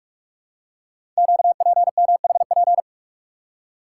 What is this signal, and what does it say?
Morse code, heard as a keyed tone, about 44 wpm, 690 Hz. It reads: QPMHP
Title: Code Groups - 5 Letters - 45wpm